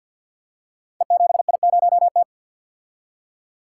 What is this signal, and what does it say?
Morse code, heard as a keyed tone, about 47 wpm, 700 Hz. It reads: E7I0T
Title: Code Groups - 5 Letter:Numbers - 50wpm